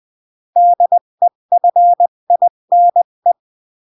Morse code, heard as a keyed tone, about 20 wpm, 700 Hz. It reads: DEFINE
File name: CW / jodorowsky / 82 commandments 20-15